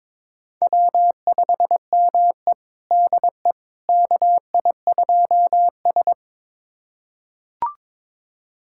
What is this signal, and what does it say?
Morse code, heard as a keyed tone, about 22 wpm, 700 Hz. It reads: W5ME DE KI2S
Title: USA Callsign DE Callsign - 4 Characters - 22wpm